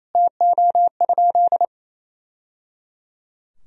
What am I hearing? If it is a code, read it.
Morse code, heard as a keyed tone, about 28 wpm, 700 Hz. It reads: TO?
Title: Sentences from Top 1000 Words - Review Part 4 - 28wpm